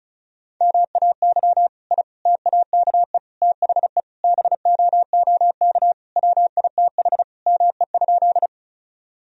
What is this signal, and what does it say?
Morse code, heard as a keyed tone, about 36 wpm, 700 Hz. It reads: MAY I TAKE THE BOOK WITH ME?
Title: Sentences from Top 1000 Words - Review Part 1 - 35wpm